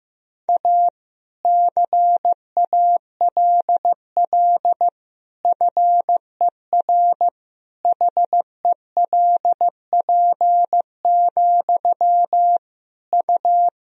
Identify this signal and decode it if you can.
Morse code, heard as a keyed tone, about 15 wpm, 700 Hz. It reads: A CALL FER HELP, U
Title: Sentences from Top 300 Words and Abbreviations 15wpm